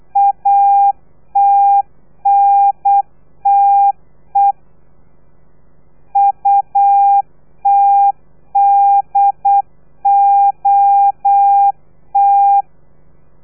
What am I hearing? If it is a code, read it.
Morse code, heard as a keyed tone, about 8 wpm, 800 Hz. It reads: ATNTE UTDOT
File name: telegrafia / i2tfj / Lez3 35